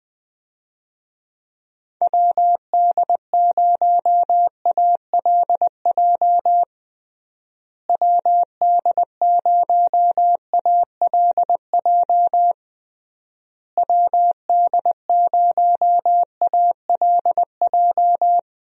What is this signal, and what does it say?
Morse code, heard as a keyed tone, about 20 wpm, 700 Hz. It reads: WD0ALJ WD0ALJ WD0ALJ
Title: USA Call Signs - 6 Characters - Repeated 3x - 20wpm